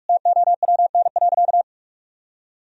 Morse code, heard as a keyed tone, about 42 wpm, 700 Hz. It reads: TOWN.
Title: Sentences from Top 1000 Words - Review Part 2 - 45wpm